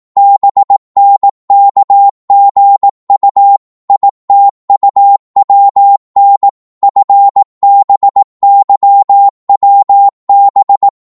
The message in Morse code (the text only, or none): BNKGU ITUWN FBYWB